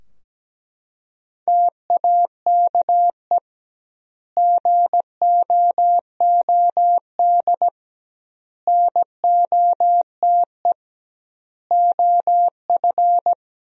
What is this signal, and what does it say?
Morse code, heard as a keyed tone, about 17 wpm, 700 Hz. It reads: TAKE GOOD NOTE OF
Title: 2x Word-Spacing - Sentences from Top 500 Words - 17wpm